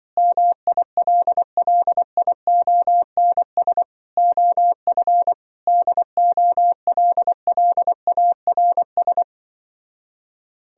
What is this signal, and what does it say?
Morse code, heard as a keyed tone, about 24 wpm, 700 Hz. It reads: MILLIONS OF DOLLARS
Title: Speed-Racing - Sets of 5 Words 20wpm